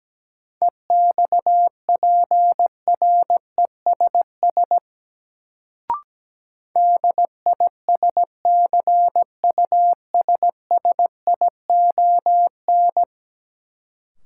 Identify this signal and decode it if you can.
Morse code, heard as a keyed tone, about 17 wpm, 700 Hz. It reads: EXPRESS DISCUSSION
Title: Sentences from Top 1100 Words - New Word Drill - 17wpm